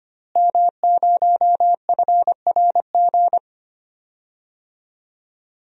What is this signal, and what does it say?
Morse code, heard as a keyed tone, about 25 wpm, 700 Hz. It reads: M0FRG
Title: DXCC-100 Call Signs - 5 Characters - Repeated 2x - 25wpm